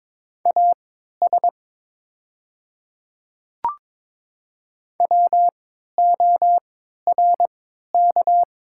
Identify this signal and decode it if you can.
Morse code, heard as a keyed tone, about 23 wpm, spaced as for 13 wpm, 700 Hz. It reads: AS WORK
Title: 3x Character Spacing - Top 200 Words - 22wpm